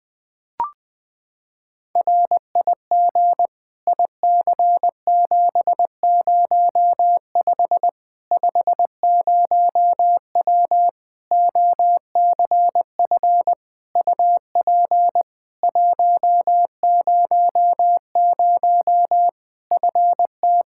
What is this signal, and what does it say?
Morse code, heard as a keyed tone, about 20 wpm, 700 Hz. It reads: RIG IC705 50W OCF UP 100 FT
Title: Practice QSO - Exchange 2 - Info - 20wpm